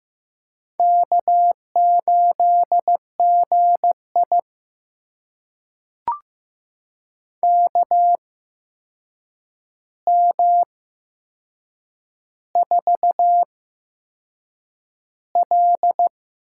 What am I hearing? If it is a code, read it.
Morse code, heard as a keyed tone, about 15 wpm, spaced as for 6 wpm, 700 Hz. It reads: K8GI KM4L
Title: USA Call Signs - 8x Character Spacing - 15wpm